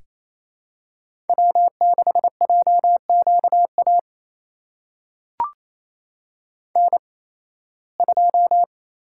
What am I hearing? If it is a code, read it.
Morse code, heard as a keyed tone, about 28 wpm, spaced as for 18 wpm, 700 Hz. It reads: W6JQA N2
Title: USA Call Signs - 5 Characters - 8x Character Spacing - 28wpm